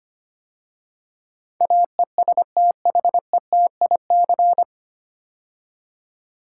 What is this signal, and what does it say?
Morse code, heard as a keyed tone, about 25 wpm, 700 Hz. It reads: AESTHETIC
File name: practice / 25wpm / 9 Letter Words - First 3 Letters Given - 25wpm